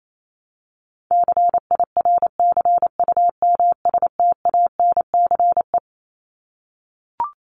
Morse code, heard as a keyed tone, about 28 wpm, 700 Hz. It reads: CIRCUMSTANCE
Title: ES:S Suffix - Top 100 Words with Base Words 28wpm